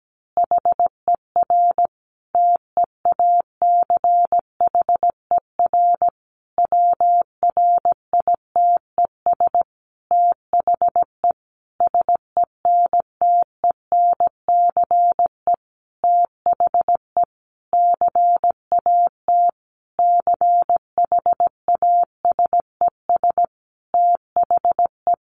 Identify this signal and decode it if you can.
Morse code, heard as a keyed tone, about 17 wpm, 700 Hz. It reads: HER TEACHER WRITES THE SENTENCE THE CAT CHASES THE
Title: Additional Easy Phrases 17wpm